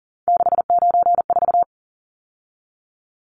Morse code, heard as a keyed tone, about 40 wpm, 700 Hz. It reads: 694
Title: Numbers - 3-Digit 40wpm